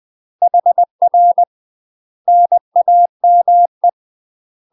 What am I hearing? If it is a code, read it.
Morse code, heard as a keyed tone, about 20 wpm, 700 Hz. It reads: HR NAME
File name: UK MASTER QSOs TRAINING / 20WPM / UK MASTER QSOs 0070